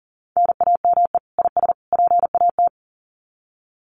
Morse code, heard as a keyed tone, about 39 wpm, 700 Hz. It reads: NAME IS PAT
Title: Common QSO Phrases 40wpm